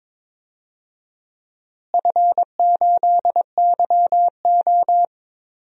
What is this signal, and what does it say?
Morse code, heard as a keyed tone, about 22 wpm, 700 Hz. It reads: F8YO
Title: DXCC-100 Call Signs - 4 Characters - Repeated 2x - 22wpm